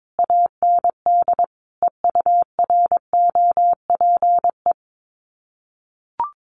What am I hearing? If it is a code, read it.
Morse code, heard as a keyed tone, about 22 wpm, 700 Hz. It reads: AND EUROPE
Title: Sets of 5 Words Encore 22wpm